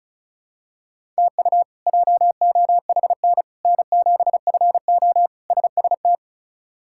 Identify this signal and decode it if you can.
Morse code, heard as a keyed tone, about 36 wpm, 700 Hz. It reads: TU JOHN N7FO SST
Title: K1USN SST QSO Practice - 35wpm